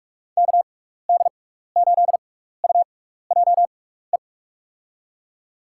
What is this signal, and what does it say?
Morse code, heard as a keyed tone, about 46 wpm, spaced as for 16 wpm, 700 Hz. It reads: KD8UJE
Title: USA Call Signs - 6 Characters - 6x Character Spacing - 45wpm